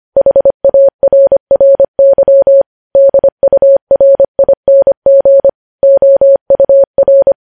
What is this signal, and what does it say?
Morse code, heard as a keyed tone, about 25 wpm, 550 Hz. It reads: HARRY DURING OUR
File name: podcast / News headlines-1x-25x25